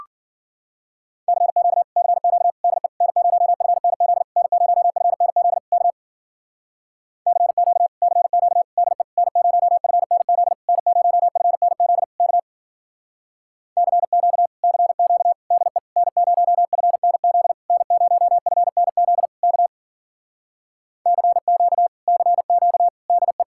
Morse code, heard as a keyed tone, about 48 wpm, 700 Hz. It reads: CQ CQ DE N0FNZ N0FNZ K CQ CQ DE N0FNZ N0FNZ K CQ CQ DE N0FNZ N0FNZ K CQ CQ DE
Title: Speed-Racing Common QSO Phrases Encore - 40wpm